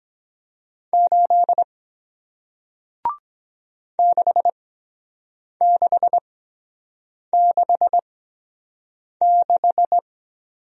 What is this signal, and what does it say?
Morse code, heard as a keyed tone, about 23 wpm, spaced as for 7 wpm, 700 Hz. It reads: 8 6666